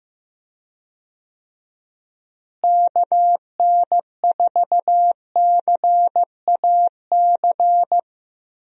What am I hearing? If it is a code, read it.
Morse code, heard as a keyed tone, about 15 wpm, 700 Hz. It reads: KN4CAC
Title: USA Call Signs - 6 Characters - Repeated 2x - 15wpm